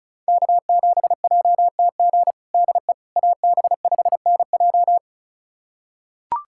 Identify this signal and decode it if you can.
Morse code, heard as a keyed tone, about 36 wpm, 700 Hz. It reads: K7JTG DE AB5NJ
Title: USA Callsign DE Callsign - 5 Characters - 35wpm